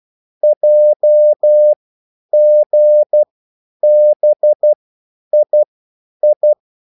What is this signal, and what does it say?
Morse code, heard as a keyed tone, about 12 wpm, spaced as for 9 wpm, 600 Hz. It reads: JGBII